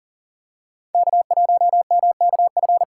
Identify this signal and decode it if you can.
Morse code, heard as a keyed tone, about 40 wpm, 700 Hz. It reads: K1MKF
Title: USA Call Signs - 5 Characters - 4x Character Spacing - 40wpm